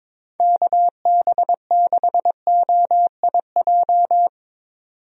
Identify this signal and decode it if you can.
Morse code, heard as a keyed tone, about 22 wpm, 700 Hz. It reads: KB6OIJ
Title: USA Call Signs - 6 Characters - 22wpm